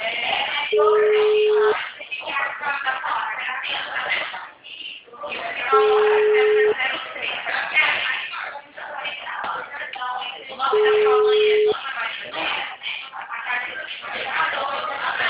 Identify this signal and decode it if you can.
Ringback tone (Europe, 425 Hz), with speech.
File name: out- -2001- - - .83925